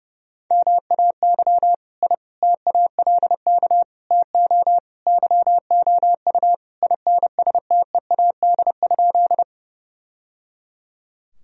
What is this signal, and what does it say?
Morse code, heard as a keyed tone, about 32 wpm, 700 Hz. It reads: MAY I TALK TO YOU INSTEAD?
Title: Sentences from Top 1000 Words - Review Part 4 - 30wpm